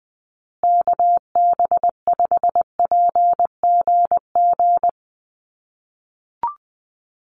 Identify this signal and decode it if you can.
Morse code, heard as a keyed tone, about 20 wpm, 700 Hz. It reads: KB5PGG
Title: USA Call Signs Repeated 2x 20wpm